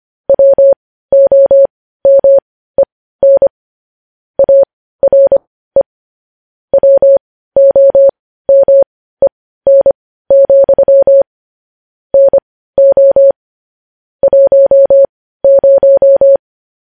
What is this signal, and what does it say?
Morse code, heard as a keyed tone, about 25 wpm, spaced as for 15 wpm, 550 Hz. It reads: WOMEN ARE WOMEN, NO 10
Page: news headlines in morse code at 15 wpm